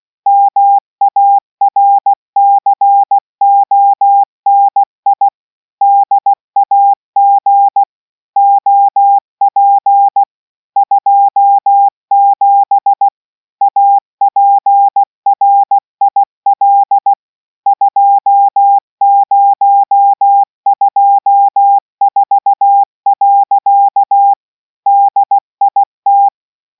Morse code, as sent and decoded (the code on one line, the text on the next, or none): -- .- .-. -.-. --- -. .. / -.. .- --. / --- .--. / ..--- --... / .- .--. .-. .. .-.. / ..--- ----- ..--- ....- .-.-.- / -.. .. -
MARCONI DAG OP 27 APRIL 2024. DIT